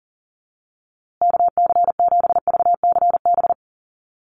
Morse code, heard as a keyed tone, about 38 wpm, 700 Hz. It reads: KC7VCB
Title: USA Call Signs Repeated 2x 40wpm